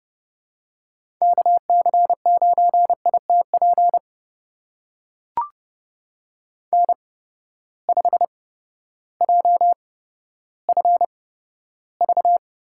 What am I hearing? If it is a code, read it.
Morse code, heard as a keyed tone, about 32 wpm, spaced as for 14 wpm, 700 Hz. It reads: KC9ITP N5JFV
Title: USA Call Signs - 8x Character Spacing - 30wpm